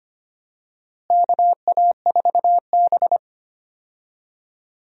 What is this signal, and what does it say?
Morse code, heard as a keyed tone, about 25 wpm, 700 Hz. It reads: KA4B
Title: USA Call Signs - 4 Characters - Repeated 3x - 25wpm